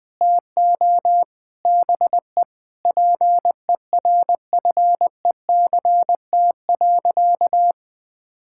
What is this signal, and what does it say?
Morse code, heard as a keyed tone, about 20 wpm, 700 Hz. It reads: TO BE PERFECT.
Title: Sentences with 5-Word Phrases - 20wpm